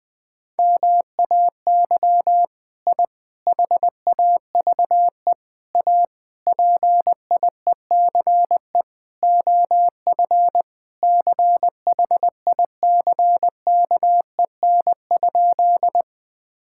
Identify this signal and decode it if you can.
Morse code, heard as a keyed tone, about 20 wpm, 700 Hz. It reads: MAY I HAVE A PIECE OF CHICKEN?
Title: Sentences from Top 1000 Words - Review Part 5 - 20wpm